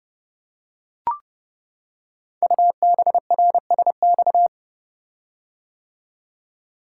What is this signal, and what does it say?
Morse code, heard as a keyed tone, about 29 wpm, 700 Hz. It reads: UBRSX
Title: Code Groups - 5 Letters - 30wpm